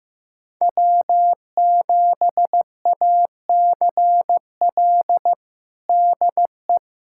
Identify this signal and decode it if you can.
Morse code, heard as a keyed tone, about 15 wpm, 700 Hz. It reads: W7ACL DE
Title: USA Callsign DE Callsign - 5 Characters - 15wpm